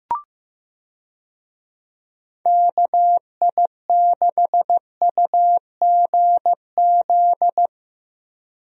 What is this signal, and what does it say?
Morse code, heard as a keyed tone, about 15 wpm, 700 Hz. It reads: KI6UGZ
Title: USA Call Signs - 6 Characters - Repeated 2x - 15wpm